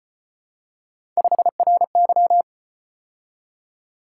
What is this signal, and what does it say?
Morse code, heard as a keyed tone, about 32 wpm, 700 Hz. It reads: 5RY